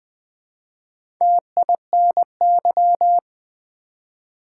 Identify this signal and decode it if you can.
Morse code, heard as a keyed tone, about 20 wpm, 700 Hz. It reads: TINY
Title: Build A Word - 1 Letter At A Time - 3 to 5-Letter Words - 20wpm